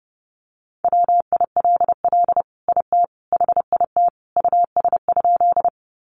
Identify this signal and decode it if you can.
Morse code, heard as a keyed tone, about 30 wpm, 700 Hz. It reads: WILL IT HIT US?
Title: Sentences from Top 1200 Words 30wpm